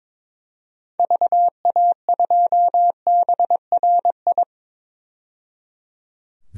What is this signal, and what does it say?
Morse code, heard as a keyed tone, about 21 wpm, 700 Hz. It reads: VA2BRI
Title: DXCC-100 Call Signs - 6 Characters - 22wpm